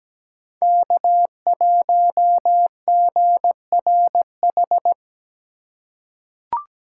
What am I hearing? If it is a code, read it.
Morse code, heard as a keyed tone, about 17 wpm, 700 Hz. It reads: K1GRH
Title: USA Call Signs - 5 Characters - 4x Character Spacing - 17wpm